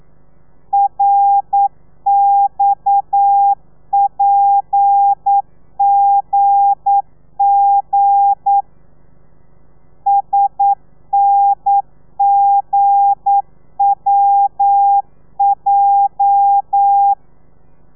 Morse code, heard as a keyed tone, about 9 wpm, 800 Hz. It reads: RXPGG SNGWJ
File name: telegrafia / i2tfj / Lez8 40